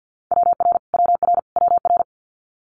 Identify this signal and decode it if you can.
Morse code, heard as a keyed tone, about 50 wpm, 700 Hz. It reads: WR WR WR
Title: Multi-letter Phonemes Repeated 3x 50wpm